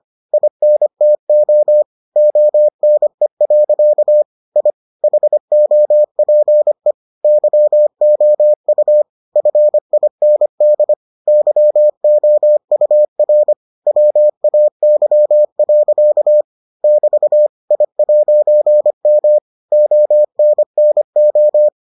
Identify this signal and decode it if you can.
Morse code, heard as a keyed tone, about 25 wpm, 600 Hz. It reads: INTO ONE. I HOPE YOU FIND YOUR WAY. <BT> I'M ONNO